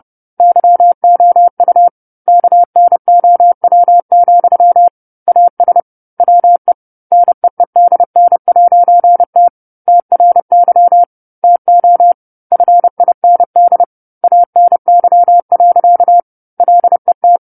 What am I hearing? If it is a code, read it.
Morse code, heard as a keyed tone, about 30 wpm, 700 Hz. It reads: YOU KNOW, AS WE NEEDN'T TRY TO FIND ANY. LET
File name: 10-Chapter-10 30 WPM